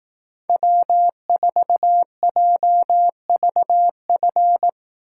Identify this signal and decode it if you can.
Morse code, heard as a keyed tone, about 18 wpm, 700 Hz. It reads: W4JVF
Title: Speed Racing - USA Call Signs 15wpm